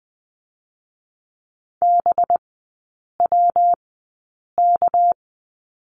Morse code, heard as a keyed tone, about 20 wpm, 700 Hz. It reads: B W K
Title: 198 Lesson 32e - Letter k - Three Characters 20wpm